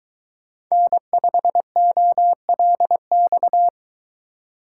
Morse code, heard as a keyed tone, about 22 wpm, 700 Hz. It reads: N5OLX